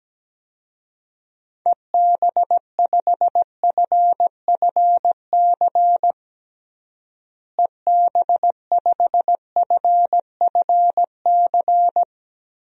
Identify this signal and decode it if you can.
Morse code, heard as a keyed tone, about 17 wpm, 700 Hz. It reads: EB5FFC EB5FFC
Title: DXCC-100 Call Signs - 6 Characters - Repeated 2x - 17wpm